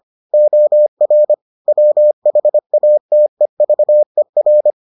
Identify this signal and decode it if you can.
Morse code, heard as a keyed tone, about 25 wpm, 600 Hz. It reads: OR WHATEVER